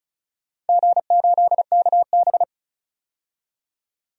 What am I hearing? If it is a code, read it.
Morse code, heard as a keyed tone, about 36 wpm, 700 Hz. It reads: G8KB
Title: DXCC-100 Call Signs - 4 Characters - 35wpm